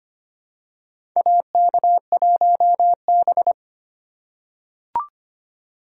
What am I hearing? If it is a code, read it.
Morse code, heard as a keyed tone, about 25 wpm, 700 Hz. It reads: AK1B
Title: USA Call Signs - 4 Characters - 3x Character Spacing - 25wpm